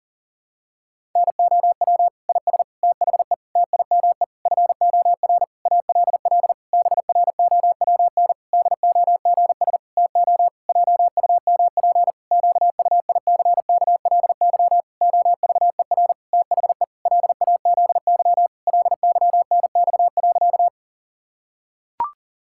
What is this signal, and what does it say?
Morse code, heard as a keyed tone, about 40 wpm, 700 Hz. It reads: NOW IS THE TIME FOR ALL BROWN DOGS TO JUMP QUICKLY OVER THE LAZY LYNX.
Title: Pangram Challenge v2 - 40wpm